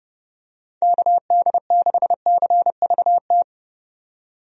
Morse code, heard as a keyed tone, about 30 wpm, 700 Hz. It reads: KD6CVT
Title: USA Call Signs - 6 Characters - 30wpm